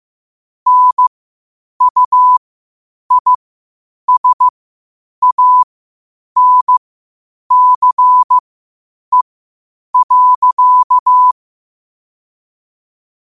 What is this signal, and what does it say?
Morse code, heard as a keyed tone, about 15 wpm, spaced as for 8 wpm, 1000 Hz. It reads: NUISANCE.
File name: qotdfiles / 8wpm